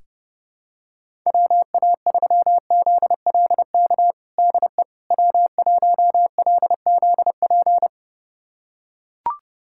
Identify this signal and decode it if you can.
Morse code, heard as a keyed tone, about 30 wpm, 700 Hz. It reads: WA3ZLK DE W1LZP
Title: USA Callsign DE Callsign 30wpm